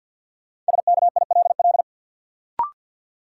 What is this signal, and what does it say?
Morse code, heard as a keyed tone, about 52 wpm, 700 Hz. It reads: SKILL
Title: Sentences from Top 1200 Words - New Word Drill Alt - 50wpm